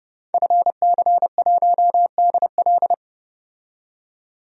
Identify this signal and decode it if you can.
Morse code, heard as a keyed tone, about 32 wpm, 700 Hz. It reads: FC1DL
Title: DXCC-100 Call Signs - 5 Characters - 30wpm